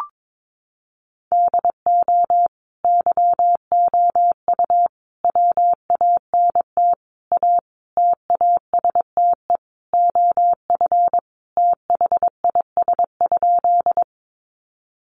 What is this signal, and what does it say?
Morse code, heard as a keyed tone, about 22 wpm, 700 Hz. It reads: DO YOU WANT A TASTE OF THIS?
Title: Sentences from Top 1100 Words 22wpm